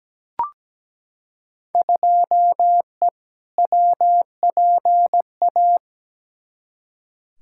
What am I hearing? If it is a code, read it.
Morse code, heard as a keyed tone, about 17 wpm, 700 Hz. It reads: 2E WPA
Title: Field Day Exchanges - 17wpm